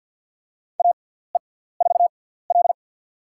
Morse code, heard as a keyed tone, about 50 wpm, spaced as for 17 wpm, 700 Hz. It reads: AE4L
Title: USA Call Signs - 6x Character Spacing - 50wpm